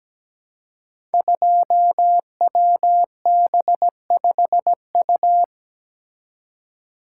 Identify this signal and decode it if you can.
Morse code, heard as a keyed tone, about 17 wpm, 700 Hz. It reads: 2WB5U
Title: Code Groups - 5 Letter:Numbers - 17wpm